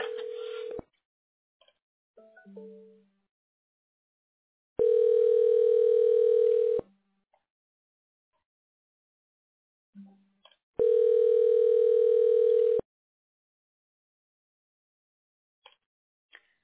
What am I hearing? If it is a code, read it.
Ringback tone (North America), 440 + 480 Hz.